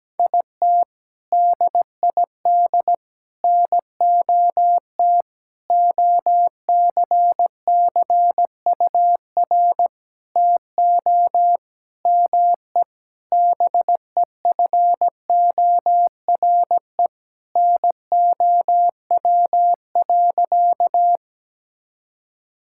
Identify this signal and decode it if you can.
Morse code, heard as a keyed tone, about 17 wpm, 700 Hz. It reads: IT DID NOT OCCUR TO ME BEFORE NOW.
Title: Sentences from Top 1000 Words - Review Part 6 - 17wpm